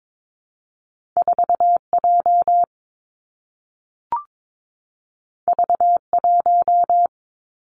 Morse code, heard as a keyed tone, about 22 wpm, 700 Hz. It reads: 4J V1
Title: Two Challenging Letter-Number Combinations 22wpm